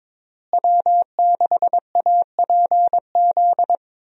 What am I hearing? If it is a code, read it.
Morse code, heard as a keyed tone, about 21 wpm, 700 Hz. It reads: W6APZ
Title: USA Call Signs - 5 Characters - Repeated 3x - 22wpm